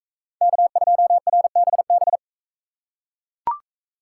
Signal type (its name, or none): Morse code